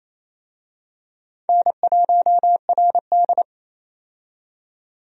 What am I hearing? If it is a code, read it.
Morse code, heard as a keyed tone, about 28 wpm, 700 Hz. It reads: N1RD
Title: USA Call Signs - 4 Characters - 28wpm